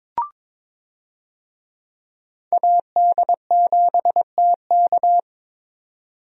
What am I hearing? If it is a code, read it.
Morse code, heard as a keyed tone, about 22 wpm, 700 Hz. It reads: AD7TK
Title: USA Call Signs - 5 Characters - 22wpm